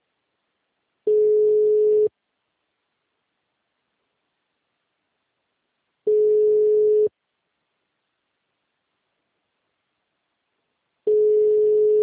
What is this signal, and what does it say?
Ringback tone (Europe, 425 Hz).